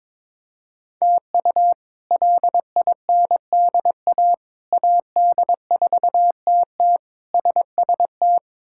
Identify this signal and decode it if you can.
Morse code, heard as a keyed tone, about 22 wpm, 700 Hz. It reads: TU LINDA AD4TT SST
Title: K1USN SST QSO Practice - 22wpm